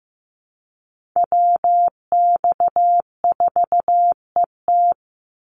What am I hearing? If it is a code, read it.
Morse code, heard as a keyed tone, about 15 wpm, 700 Hz. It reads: WX4ET
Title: NA SOTA Chaser Callsigns 15wpm